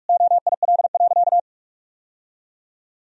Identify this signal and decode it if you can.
Morse code, heard as a keyed tone, about 45 wpm, 700 Hz. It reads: OIL.
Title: Sentences from Top 1000 Words - Review Part 4 - 45wpm